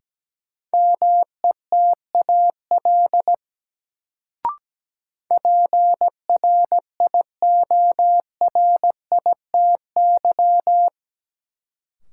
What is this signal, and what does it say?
Morse code, heard as a keyed tone, about 17 wpm, 700 Hz. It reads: METAL PRIORITY
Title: Sentences from Top 1600 Words - New Word Drill Alt - 17wpm